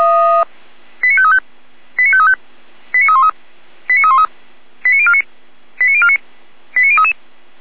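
Five-tone selective call (ZVEI), heard as quick runs of five tones, tones of about 70 ms. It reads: 89535 89535 89313 89313 89049 89049 89033
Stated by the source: ZVEI five-tone selective call received on an SDR.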